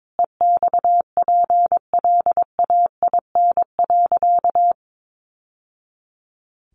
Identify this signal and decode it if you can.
Morse code, heard as a keyed tone, about 22 wpm, 700 Hz. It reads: EXPLAIN.
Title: Sentences from Top 1100 Words 22wpm